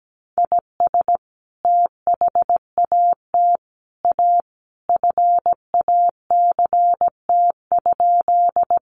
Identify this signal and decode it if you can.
Morse code, heard as a keyed tone, about 17 wpm, 700 Hz. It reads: IS THAT A FACT?
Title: Sentences from Top 500 Words 17wpm